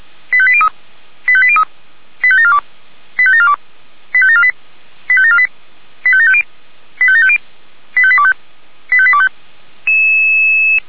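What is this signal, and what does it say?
Five-tone selective call (ZVEI), heard as quick runs of five tones, tones of about 70 ms. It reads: 85803 85803 85742 85742 85748 85748 85750 85750 85825 85825
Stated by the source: ZVEI five-tone selective call 85827 received on SDR.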